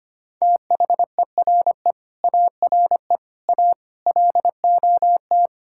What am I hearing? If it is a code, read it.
Morse code, heard as a keyed tone, about 25 wpm, 700 Hz. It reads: THERE ARE A LOT
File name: Sets of 9 Words - 25wpm